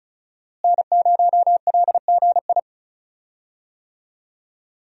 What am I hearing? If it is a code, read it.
Morse code, heard as a keyed tone, about 34 wpm, 700 Hz. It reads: N0LGI
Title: USA Call Signs - 5 Characters - Repeated 2x - 35wpm